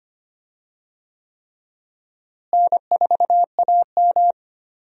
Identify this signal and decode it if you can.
Morse code, heard as a keyed tone, about 25 wpm, 700 Hz. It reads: N4AM
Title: USA Call Signs - 4 Characters - Repeated 3x - 25wpm